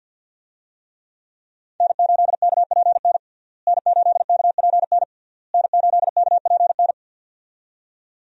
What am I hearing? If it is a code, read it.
Morse code, heard as a keyed tone, about 51 wpm, 700 Hz. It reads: N8KPN N8KPN N8KPN
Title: USA Call Signs - 5 Characters - Repeated 3x - 50wpm